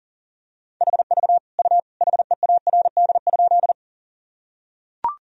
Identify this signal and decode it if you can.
Morse code, heard as a keyed tone, about 40 wpm, 700 Hz. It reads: HV U HEARD?
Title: Sentences from Top 400 Words and Abbreviations 40wpm